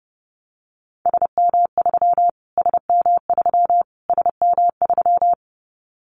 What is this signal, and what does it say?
Morse code, heard as a keyed tone, about 30 wpm, 700 Hz. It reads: SM3 SM3 SM3
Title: Callsign Prefixes for Top 25 DXCC - Repeated 3x - 30wpm